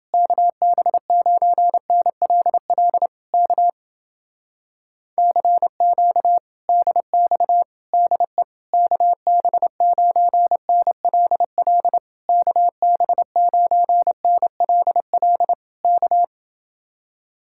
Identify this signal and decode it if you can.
Morse code, heard as a keyed tone, about 28 wpm, 700 Hz. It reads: KB9NLL K CQ DX DE KB9NLL KB9NLL K